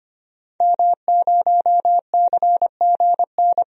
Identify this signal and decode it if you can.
Morse code, heard as a keyed tone, about 25 wpm, 700 Hz. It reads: M0CGN
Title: DXCC-100 Call Signs - 5 Characters - Repeated 3x - 25wpm